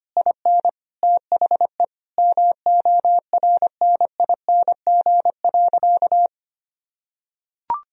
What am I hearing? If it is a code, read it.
Morse code, heard as a keyed tone, about 25 wpm, 700 Hz. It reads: IN THE MORNING.
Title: Sentences from Top 1000 Words - Review Part 2 - 25wpm